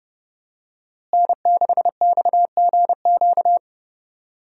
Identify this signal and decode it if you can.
Morse code, heard as a keyed tone, about 30 wpm, 700 Hz. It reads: N6XGQ
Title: USA Call Signs - 5 Characters - Repeated 3x - 30wpm